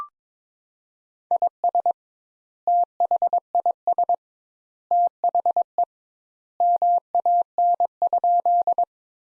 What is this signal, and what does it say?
Morse code, heard as a keyed tone, about 21 wpm, 700 Hz. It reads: IS THIS THE MAN?
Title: Sentences Top 200 Words - 22wpm